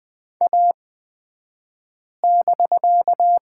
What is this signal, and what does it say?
Morse code, heard as a keyed tone, about 20 wpm, spaced as for 5 wpm, 700 Hz. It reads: A<BK>
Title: 249 Lesson 40d - BK - Two Characters 20wpm